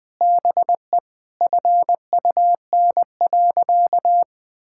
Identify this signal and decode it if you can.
Morse code, heard as a keyed tone, about 20 wpm, 700 Hz. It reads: BE FUN.
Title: Sentences with 5-Word Phrases - 20wpm